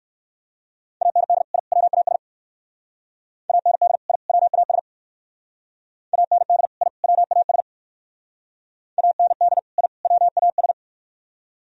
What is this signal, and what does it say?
Morse code, heard as a keyed tone, about 55 wpm, 700 Hz. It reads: AND I WAS AND I WAS AND I WAS AND I WAS
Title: Speed-Racing - Sets of 3 Words 45wpm